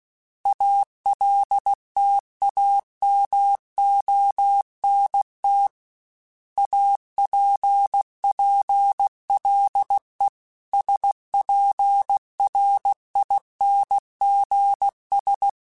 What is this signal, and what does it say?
Morse code, heard as a keyed tone, about 16 wpm, 800 Hz. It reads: ALTAMONT APPLE SPRINGS
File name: fromSMs / FromSMs mp3s / Cities@15